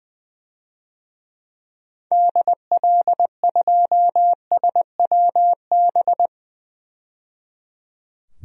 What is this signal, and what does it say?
Morse code, heard as a keyed tone, about 20 wpm, 700 Hz. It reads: DL2SWB
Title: DXCC-100 Call Signs - 6 Characters - 20wpm